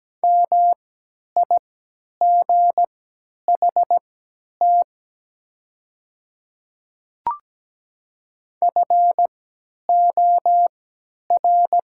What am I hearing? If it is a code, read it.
Morse code, heard as a keyed tone, about 17 wpm, spaced as for 10 wpm, 700 Hz. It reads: MIGHT FOR
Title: 3x Character Spacing - Top 300 Words - 17wpm